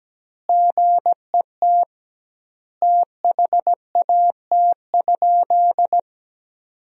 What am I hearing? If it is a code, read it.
Morse code, heard as a keyed tone, about 17 wpm, 700 Hz. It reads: GET THAT?
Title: 2x Word-Spacing - Sentences from Top 200 Words - 17wpm